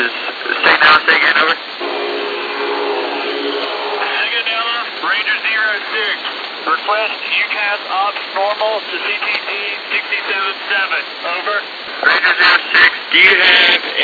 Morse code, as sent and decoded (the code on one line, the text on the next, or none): none